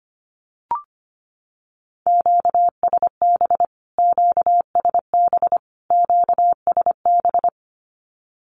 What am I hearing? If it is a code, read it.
Morse code, heard as a keyed tone, about 25 wpm, 700 Hz. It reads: QSB QSB QSB
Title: Q-Codes Repeated 3x 25wpm